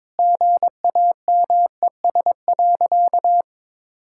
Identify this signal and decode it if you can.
Morse code, heard as a keyed tone, about 22 wpm, 690 Hz. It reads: GAMES.
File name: practice / 22wpm / Easy Sentences with Hints - 22wpm